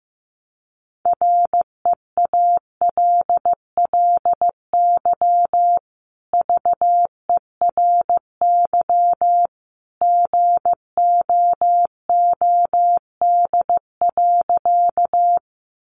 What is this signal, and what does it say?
Morse code, heard as a keyed tone, about 15 wpm, 700 Hz. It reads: REALLY VERY GOOD.
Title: Sentences from Top 600 Words 15wpm